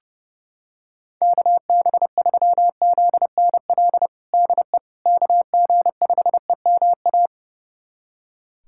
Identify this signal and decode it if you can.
Morse code, heard as a keyed tone, about 30 wpm, 700 Hz. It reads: KB3ZNL DE KG5EMA
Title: USA Callsign DE Callsign - 6 Characters - 30wpm